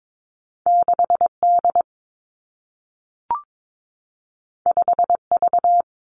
Morse code, heard as a keyed tone, about 22 wpm, 700 Hz. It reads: 6 D 5 V
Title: Two Challenging Letter-Number Combinations 22wpm